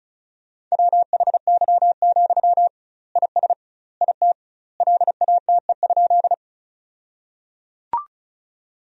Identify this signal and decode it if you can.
Morse code, heard as a keyed tone, about 34 wpm, 700 Hz. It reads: WHY, IS IT LATE?
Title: 2x Word-Spacing - Sentences from Top 300 Words - 35wpm